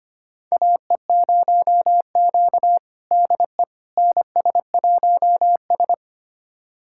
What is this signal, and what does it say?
Morse code, heard as a keyed tone, about 25 wpm, 700 Hz. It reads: AE0Q DE NS1S
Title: USA Callsign DE Callsign - 4 Characters - 25wpm